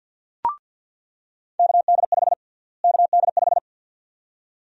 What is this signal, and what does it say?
Morse code, heard as a keyed tone, about 48 wpm, 700 Hz. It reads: KD5 KD5
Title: USA Call Sign Prefixes - Repeated 2x - 50wpm